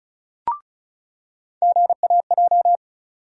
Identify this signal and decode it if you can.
Morse code, heard as a keyed tone, about 36 wpm, 700 Hz. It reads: GAJ
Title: USA Call Sign Postfixes - 3 Letters 35wpm